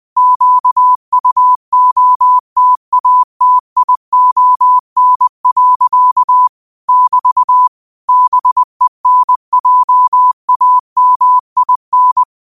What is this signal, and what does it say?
Morse code, heard as a keyed tone, about 20 wpm, 1000 Hz. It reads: QUOTATION. <BT> BENJAMIN
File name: qotdfiles / 20wpm